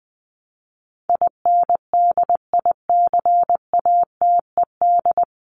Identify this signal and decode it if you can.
Morse code, heard as a keyed tone, about 20 wpm, 700 Hz. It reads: INDICATED
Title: ED Suffix - Top 100 Words with Base Words 20wpm